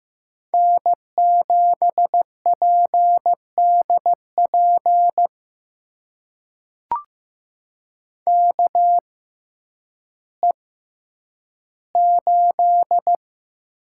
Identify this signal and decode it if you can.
Morse code, heard as a keyed tone, about 15 wpm, spaced as for 9 wpm, 700 Hz. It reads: N7PDP KE8
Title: USA Call Signs - 5 Characters - 6x Character Spacing - 15wpm